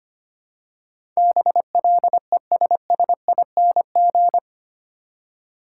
Morse code, heard as a keyed tone, about 25 wpm, 700 Hz. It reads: BLESSING
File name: Eight Letter Words - 25wpm